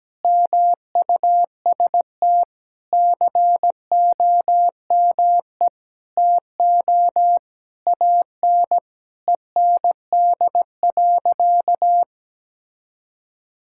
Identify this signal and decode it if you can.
Morse code, heard as a keyed tone, about 17 wpm, 700 Hz. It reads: MUST COME TO AN END.
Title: Sentences from Top 1000 Words - Review Part 5 - 17wpm